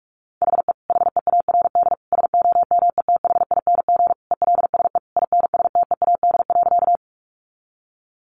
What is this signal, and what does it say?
Morse code, heard as a keyed tone, about 44 wpm, 700 Hz. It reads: HE HEARD SOMETHING ELSE INSTEAD.
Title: Sentences from Top 800 Words 45wpm